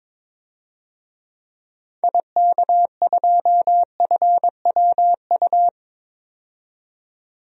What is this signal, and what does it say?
Morse code, heard as a keyed tone, about 22 wpm, 700 Hz. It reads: IK2FWU
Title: DXCC-100 Call Signs - 6 Characters - 22wpm